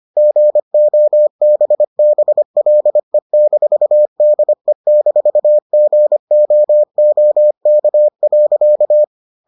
Morse code, heard as a keyed tone, about 25 wpm, 600 Hz. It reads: GOBBLE-DE-GOOK.